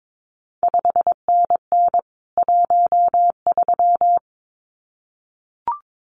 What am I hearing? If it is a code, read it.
Morse code, heard as a keyed tone, about 22 wpm, 700 Hz. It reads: 5NN 13
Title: CQ WW DX Contest Exchange Encore - 22wpm